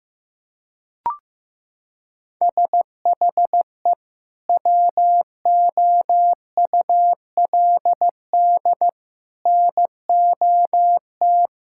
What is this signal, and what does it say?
Morse code, heard as a keyed tone, about 15 wpm, 700 Hz. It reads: SHE WOULD NOT
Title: Sentences from Top 1600 Words 15wpm